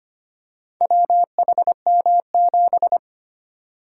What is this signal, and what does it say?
Morse code, heard as a keyed tone, about 25 wpm, 700 Hz. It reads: WHM7
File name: practice / 25wpm / Four Letter-Number Combinations - 25wpm